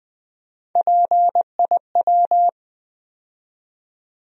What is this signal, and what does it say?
Morse code, heard as a keyed tone, about 20 wpm, 700 Hz. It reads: PIW
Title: USA Call Sign Postfixes - 3 Letters 20wpm